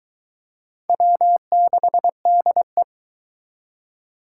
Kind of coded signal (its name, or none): Morse code